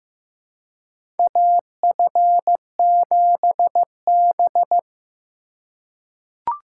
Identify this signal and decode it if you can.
Morse code, heard as a keyed tone, about 15 wpm, 700 Hz. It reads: AF7B
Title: USA Call Signs - 4 Characters - 3x Character Spacing - 15wpm